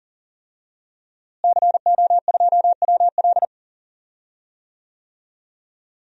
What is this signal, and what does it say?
Morse code, heard as a keyed tone, about 40 wpm, 700 Hz. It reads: CO2WL
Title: DXCC-100 Call Signs - 5 Characters - Repeated 3x - 40wpm